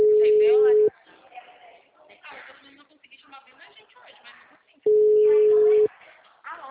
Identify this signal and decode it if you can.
Ringback tone (Europe, 425 Hz), with speech.